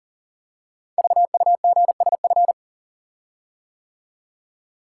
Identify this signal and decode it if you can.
Morse code, heard as a keyed tone, about 39 wpm, 700 Hz. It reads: VUGSF
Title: Code Groups - 5 Letters - 40wpm